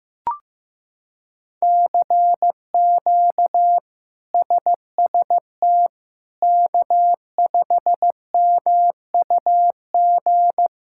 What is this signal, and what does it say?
Morse code, heard as a keyed tone, about 15 wpm, 700 Hz. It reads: CQ SST K5MUG
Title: K1USN SST QSO Practice - 15wpm